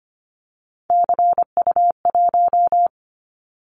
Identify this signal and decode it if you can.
Morse code, heard as a keyed tone, about 25 wpm, 700 Hz. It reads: CU1
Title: Callsign Prefixes for Top 100 DXCC - 25wpm